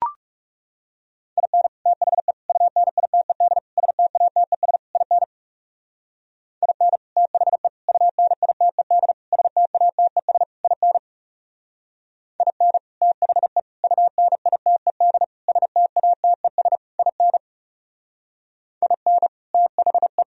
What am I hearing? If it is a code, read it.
Morse code, heard as a keyed tone, about 36 wpm, 700 Hz. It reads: IN THE UNITED STATES IN IN THE UNITED STATES IN IN THE UNITED STATES IN IN THE
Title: Speed-Racing - Sets of 5 Words 30wpm